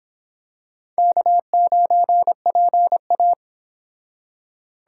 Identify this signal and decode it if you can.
Morse code, heard as a keyed tone, about 26 wpm, 700 Hz. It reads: K9PA